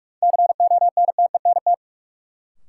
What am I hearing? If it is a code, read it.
Morse code, heard as a keyed tone, about 42 wpm, 700 Hz. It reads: CONTENT
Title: Sentences from Top 900 Words - New Word Drill - 45wpm